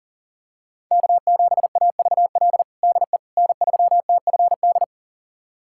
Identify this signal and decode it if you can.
Morse code, heard as a keyed tone, about 40 wpm, 700 Hz. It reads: K7AVL DE N3TFD
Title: USA Callsign DE Callsign - 5 Characters - 40wpm